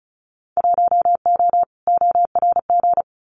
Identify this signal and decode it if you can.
Morse code, heard as a keyed tone, about 34 wpm, 700 Hz. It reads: 1O ORG
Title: Winter Field Day Exchanges 35wpm